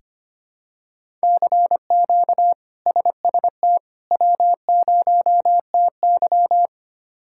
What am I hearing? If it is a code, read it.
Morse code, heard as a keyed tone, about 25 wpm, 700 Hz. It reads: CQ SST W0TY
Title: K1USN SST QSO Practice - 25wpm